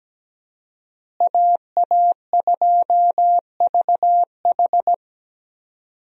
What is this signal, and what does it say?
Morse code, heard as a keyed tone, about 17 wpm, 700 Hz. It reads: AA2VH
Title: USA Call Signs - 5 Characters - Repeated 3x - 17wpm